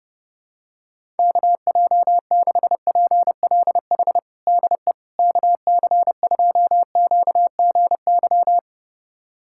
Morse code, heard as a keyed tone, about 30 wpm, 700 Hz. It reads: KJ6PLH DE KC2QGY
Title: USA Callsign DE Callsign - 6 Characters - 30wpm